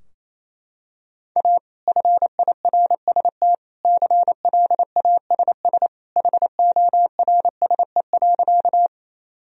Morse code, heard as a keyed tone, about 28 wpm, 700 Hz. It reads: A FIRST CLASS HORSE.
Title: Sentences from Top 1000 Words - Review Part 2 - 28wpm